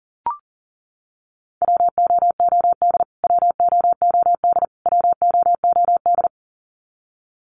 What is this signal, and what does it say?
Morse code, heard as a keyed tone, about 40 wpm, 700 Hz. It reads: WOOD WOOD WOOD
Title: Top 500 Words Repeated 40wpm